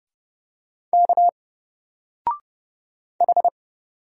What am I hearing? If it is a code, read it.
Morse code, heard as a keyed tone, about 31 wpm, 700 Hz. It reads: K H
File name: Single-Letter-Number-30wpm